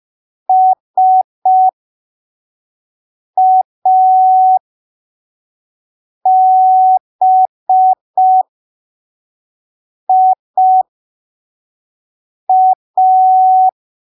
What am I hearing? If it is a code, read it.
Morse code, heard as a keyed tone, about 5 wpm, spaced as for 3 wpm, 750 Hz. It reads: SABIA